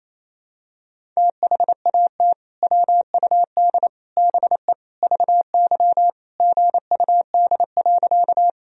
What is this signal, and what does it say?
Morse code, heard as a keyed tone, about 28 wpm, 700 Hz. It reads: THAT WUD BE VY GUD.
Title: Sentences from Top 200 Words and Abbreviations 28wpm